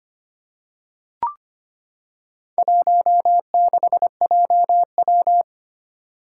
Morse code, heard as a keyed tone, about 25 wpm, 700 Hz. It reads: 16JW
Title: Four Challenging Letter-Number Combinations 25wpm